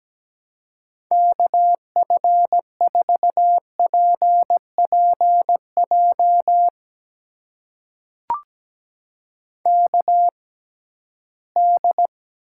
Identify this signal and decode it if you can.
Morse code, heard as a keyed tone, about 17 wpm, spaced as for 13 wpm, 700 Hz. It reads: KF4PPJ KD
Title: USA Call Signs - 6x Character Spacing - 17wpm